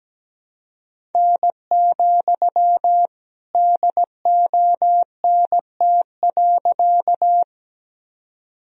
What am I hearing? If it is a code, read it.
Morse code, heard as a keyed tone, about 17 wpm, 700 Hz. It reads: N, DONT.
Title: Sentences from Top 300 Words and Abbreviations 17wpm